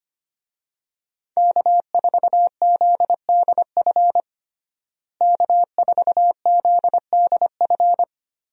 Morse code, heard as a keyed tone, about 25 wpm, 700 Hz. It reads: K4ZDF K4ZDF
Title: USA Call Signs - 5 Characters - Repeated 3x - 25wpm